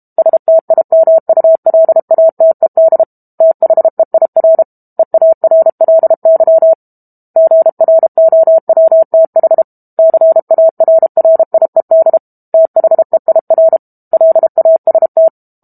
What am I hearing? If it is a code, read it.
Morse code, heard as a keyed tone, about 32 wpm, 650 Hz. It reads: STIMULATED THEIR EARLY GROWTH CARRIED THEIR LAST
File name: Book Two / 06-Chapter 6 - 32 WPM